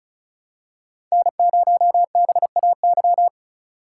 Morse code, heard as a keyed tone, about 34 wpm, 700 Hz. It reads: N0BAY
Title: USA Call Signs - 5 Characters - 4x Character Spacing - 35wpm